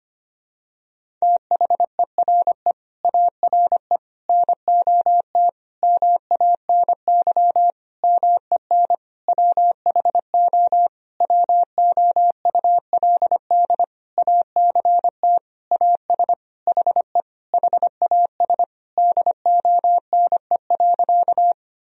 Morse code, heard as a keyed tone, about 25 wpm, 700 Hz. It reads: THERE ARE NOT MANY MEN WHO WOULD ACT AS HE HAS DONE.
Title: Sentences from Top 1000 Words - Review Part 1 - 25wpm